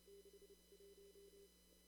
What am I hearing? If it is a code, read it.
Morse code, heard as a keyed tone, about 30 wpm, 380 Hz, with noise. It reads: 6 1
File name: CW / kt5j wpx cw-op-k5pi